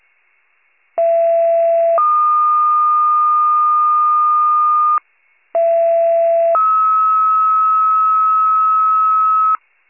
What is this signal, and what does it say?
Two-tone page, heard as a one-second tone, then a three-second tone, about 670 then 1220 Hz, about 670 then 1290 Hz. It is heard from 1.0 to 9.6 s.